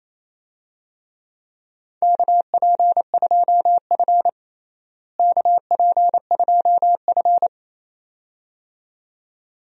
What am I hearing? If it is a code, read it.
Morse code, heard as a keyed tone, about 27 wpm, 700 Hz. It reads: KP2F KP2F
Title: DXCC-100 Call Signs - 4 Characters - Repeated 2x - 28wpm